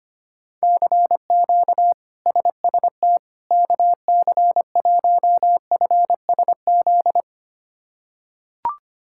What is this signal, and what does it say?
Morse code, heard as a keyed tone, about 25 wpm, 700 Hz. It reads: CQ SST KC1FSZ
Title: K1USN SST QSO Practice - 25wpm